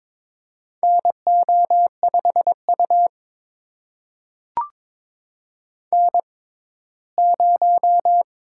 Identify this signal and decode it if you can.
Morse code, heard as a keyed tone, about 22 wpm, spaced as for 15 wpm, 700 Hz. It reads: NO5U N0
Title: USA Call Signs - 4 Characters - 6x Character Spacing - 22wpm